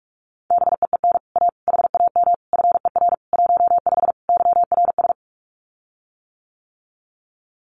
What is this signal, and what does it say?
Morse code, heard as a keyed tone, about 44 wpm, 700 Hz. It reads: BEEN A HAM FER 15 YRS
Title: Common QSO Phrases Encore 45wpm